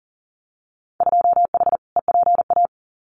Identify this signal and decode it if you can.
Morse code, heard as a keyed tone, about 40 wpm, 700 Hz. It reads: 2H EPA
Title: Winter Field Day Exchanges 40wpm